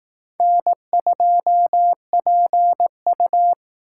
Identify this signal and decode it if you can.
Morse code, heard as a keyed tone, about 18 wpm, 700 Hz. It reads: N2PU
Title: Speed Racing - USA Call Signs 15wpm